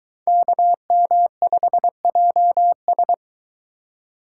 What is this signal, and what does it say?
Morse code, heard as a keyed tone, about 22 wpm, 700 Hz. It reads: KM5JS